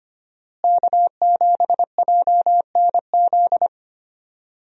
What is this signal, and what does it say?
Morse code, heard as a keyed tone, about 25 wpm, 700 Hz. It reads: K7JNZ
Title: USA Call Signs - 5 Characters - Repeated 3x - 25wpm